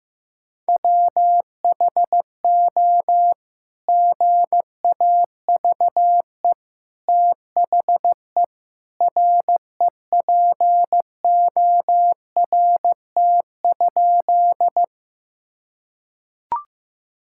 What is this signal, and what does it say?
Morse code, heard as a keyed tone, about 15 wpm, 700 Hz. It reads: WHO GAVE THE REPORT?
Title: Sentences from Top 1000 Words - Review Part 6 - 15wpm